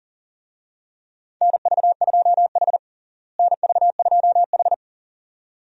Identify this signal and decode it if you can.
Morse code, heard as a keyed tone, about 40 wpm, 700 Hz. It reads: NV2H NV2H
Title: USA Call Signs - 4 Characters - Repeated 2x - 40wpm